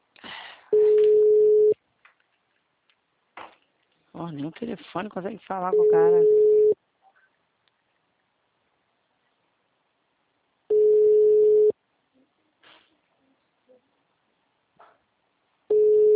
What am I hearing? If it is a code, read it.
Ringback tone (Europe, 425 Hz), with speech.